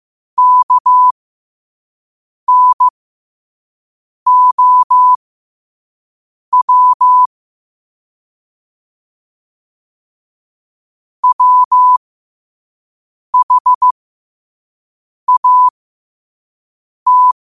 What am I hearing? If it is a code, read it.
Morse code, heard as a keyed tone, about 15 wpm, spaced as for 5 wpm, 1000 Hz. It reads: KNOW WHAT
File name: qotdfiles / 5wpm